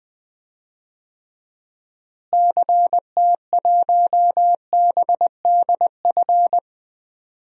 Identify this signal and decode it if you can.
Morse code, heard as a keyed tone, about 20 wpm, 700 Hz. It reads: CT1BDF
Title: DXCC-100 Call Signs - 6 Characters - Repeated 3x - 20wpm